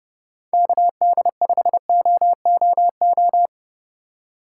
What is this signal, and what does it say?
Morse code, heard as a keyed tone, about 31 wpm, 700 Hz. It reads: KD5OOO
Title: USA Call Signs - 6 Characters - 3x Character Spacing - 30wpm